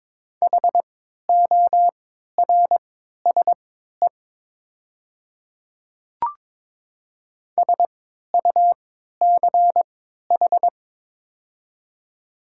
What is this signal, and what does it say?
Morse code, heard as a keyed tone, about 22 wpm, spaced as for 13 wpm, 700 Hz. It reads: HORSE SUCH
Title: 3x Character Spacing - Top 300 Words - 22wpm